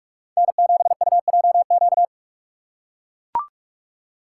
Morse code, heard as a keyed tone, about 44 wpm, 700 Hz. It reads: N7UJQ
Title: USA Call Signs - 2x Character Spacing - 45wpm